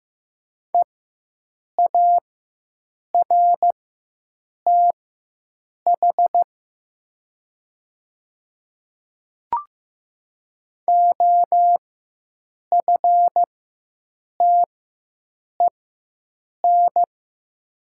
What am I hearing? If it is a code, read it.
Morse code, heard as a keyed tone, about 15 wpm, spaced as for 7 wpm, 700 Hz. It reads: EARTH OFTEN
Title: 4x Character Spacing - Top 300 Words - 15wpm